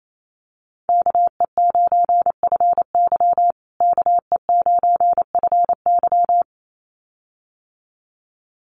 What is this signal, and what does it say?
Morse code, heard as a keyed tone, about 28 wpm, 700 Hz. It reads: KE9FY KE9FY
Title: USA Call Signs Repeated 2x 28wpm